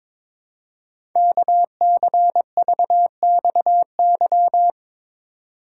Morse code, heard as a keyed tone, about 22 wpm, 700 Hz. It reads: KCVXY
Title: Code Groups - 5 Letters - 22wpm